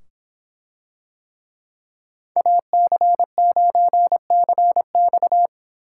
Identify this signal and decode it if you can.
Morse code, heard as a keyed tone, about 24 wpm, 700 Hz. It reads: AC9CX